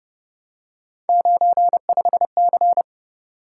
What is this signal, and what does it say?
Morse code, heard as a keyed tone, about 29 wpm, 700 Hz. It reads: 95C